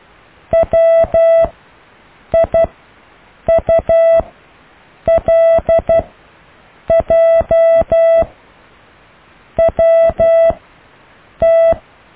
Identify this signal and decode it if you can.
Morse code, heard as a keyed tone, about 12 wpm, 660 Hz. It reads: W I U L J W T
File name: cw / IJLUM+CDRO+EKTPV+BFHNW